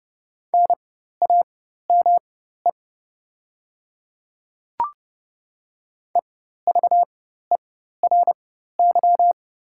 Morse code, heard as a keyed tone, about 30 wpm, spaced as for 14 wpm, 700 Hz. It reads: NAME EVERY
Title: 4x Character Spacing - Top 300 Words - 30wpm